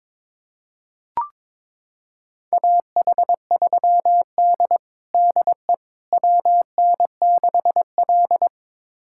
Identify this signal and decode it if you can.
Morse code, heard as a keyed tone, about 22 wpm, 700 Hz. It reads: AH3D DE WN6L
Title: USA Callsign DE Callsign - 4 Characters - 22wpm